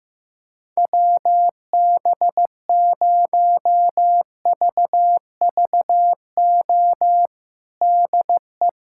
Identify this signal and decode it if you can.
Morse code, heard as a keyed tone, about 15 wpm, 700 Hz. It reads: WB0VVO DE
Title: USA Callsign DE Callsign - 6 Characters - 15wpm